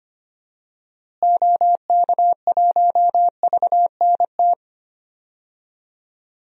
Morse code, heard as a keyed tone, about 25 wpm, 700 Hz. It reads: OK1VNT
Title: DXCC-100 Call Signs - 6 Characters - Repeated 2x - 25wpm